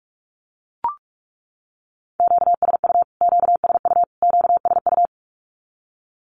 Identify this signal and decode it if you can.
Morse code, heard as a keyed tone, about 44 wpm, 700 Hz. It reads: QSU QSU QSU
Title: Q-Codes Repeated 3x 45wpm